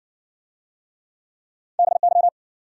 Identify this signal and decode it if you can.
Morse code, heard as a keyed tone, about 60 wpm, 700 Hz. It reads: BY